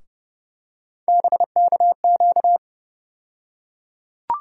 Morse code, heard as a keyed tone, about 30 wpm, 700 Hz. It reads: BKQ
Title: USA Call Sign Postfixes - 3 Letters 30wpm